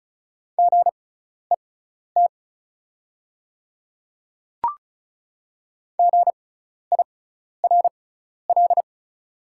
Morse code, heard as a keyed tone, about 36 wpm, spaced as for 12 wpm, 690 Hz. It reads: GET GIRL
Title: 6x Character Spacing - Top 500 Words - 35wpm